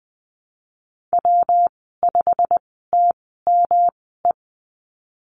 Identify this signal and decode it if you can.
Morse code, heard as a keyed tone, about 20 wpm, spaced as for 14 wpm, 700 Hz. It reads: W5TME
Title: 122 Lesson 20g - Number 3 - Call Signs 20wpm